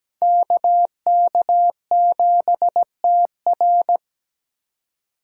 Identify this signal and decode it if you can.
Morse code, heard as a keyed tone, about 20 wpm, 700 Hz. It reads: KK7TR
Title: Speed Racing - USA 2x2 Call Signs 17wpm